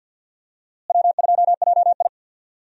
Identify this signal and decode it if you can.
Morse code, heard as a keyed tone, about 50 wpm, 700 Hz. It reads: W2JI
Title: USA Call Signs - 4 Characters - Repeated 2x - 50wpm